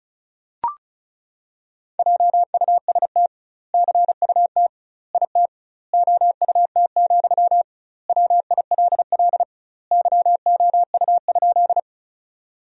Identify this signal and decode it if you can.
Morse code, heard as a keyed tone, about 34 wpm, 700 Hz. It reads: JUST CUT IT OUT, WILL YOU?
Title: 2x Word-Spacing - Sentences from Top 300 Words - 35wpm